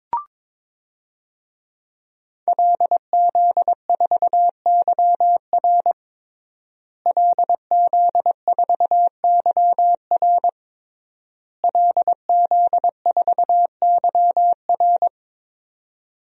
Morse code, heard as a keyed tone, about 22 wpm, 700 Hz. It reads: LZ4YR LZ4YR LZ4YR
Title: DXCC-100 Call Signs - 5 Characters - Repeated 3x - 22wpm